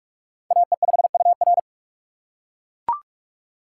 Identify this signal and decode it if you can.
Morse code, heard as a keyed tone, about 42 wpm, 700 Hz. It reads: AE5UR
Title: USA Call Signs - 5 Characters - 4x Character Spacing - 45wpm